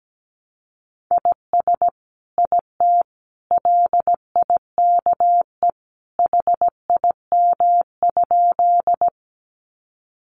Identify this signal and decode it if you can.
Morse code, heard as a keyed tone, about 17 wpm, 700 Hz. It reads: IS IT LIKE HIM?
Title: Sentences from Top 100 Words 17wpm